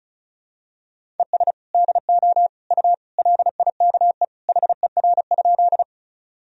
Morse code, heard as a keyed tone, about 34 wpm, 700 Hz. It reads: ES DO U LIKE HER?
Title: Sentences from Top 100 Words and Abbreviations 35wpm